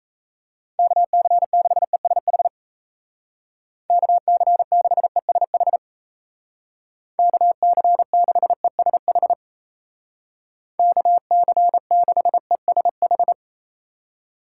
Morse code, heard as a keyed tone, about 35 wpm, 700 Hz. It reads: KC6ESH KC6ESH KC6ESH KC6ESH
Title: Speed Racing - USA Call Signs 28wpm